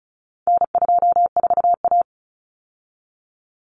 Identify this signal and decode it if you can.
Morse code, heard as a keyed tone, about 34 wpm, 700 Hz. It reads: N24A
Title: Common Cut Numbers - 4-Digit 35wpm